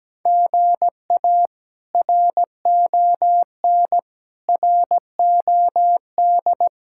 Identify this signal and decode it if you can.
Morse code, heard as a keyed tone, about 17 wpm, 700 Hz. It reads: GA RON ROD
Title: K1USN SST QSO Practice - 17wpm